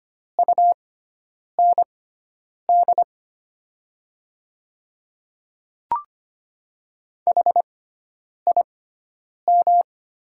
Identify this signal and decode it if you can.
Morse code, heard as a keyed tone, about 25 wpm, spaced as for 9 wpm, 700 Hz. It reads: UND HIM
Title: 6x Character Spacing - Top 100 Words - 25wpm